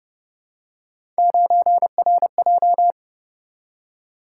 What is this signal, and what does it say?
Morse code, heard as a keyed tone, about 26 wpm, 700 Hz. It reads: 9RJ